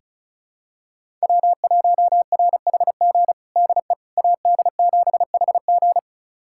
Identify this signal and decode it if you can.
Morse code, heard as a keyed tone, about 36 wpm, 700 Hz. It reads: W1RHG DE AD7HG
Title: USA Callsign DE Callsign - 5 Characters - 35wpm